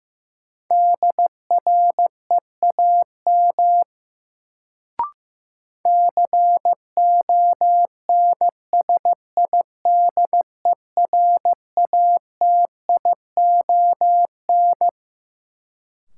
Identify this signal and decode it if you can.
Morse code, heard as a keyed tone, about 15 wpm, 700 Hz. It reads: DREAM CONSIDERATION
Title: Sentences from Top 1500 Words - New Word Drill Alt - 15wpm